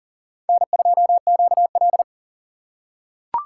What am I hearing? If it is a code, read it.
Morse code, heard as a keyed tone, about 38 wpm, 700 Hz. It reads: N2QL
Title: USA Call Signs - 4x Character Spacing - 40wpm